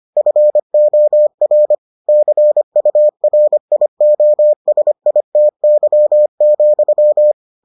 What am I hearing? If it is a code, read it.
Morse code, heard as a keyed tone, about 25 wpm, 600 Hz. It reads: FOR CURIOSITY,